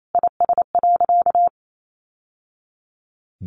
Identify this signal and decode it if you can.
Morse code, heard as a keyed tone, about 28 wpm, 700 Hz. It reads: IS.
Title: Sentences from Top 1100 Words 28wpm